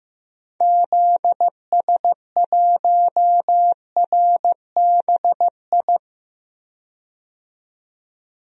Morse code, heard as a keyed tone, about 15 wpm, 700 Hz. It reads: ZS1RBI
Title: DXCC-100 Call Signs - 6 Characters - 15wpm